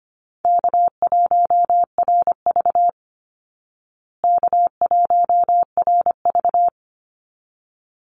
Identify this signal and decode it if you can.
Morse code, heard as a keyed tone, about 25 wpm, 700 Hz. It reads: K1RV K1RV
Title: Easy Call Signs Repeated 2x - 25wpm